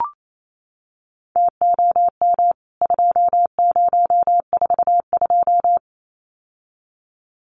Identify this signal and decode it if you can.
Morse code, heard as a keyed tone, about 28 wpm, 700 Hz. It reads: TOM 2042
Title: CWT Exchange 28wpm